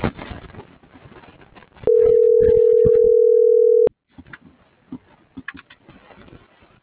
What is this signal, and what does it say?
Ringback tone (North America), 440 + 480 Hz.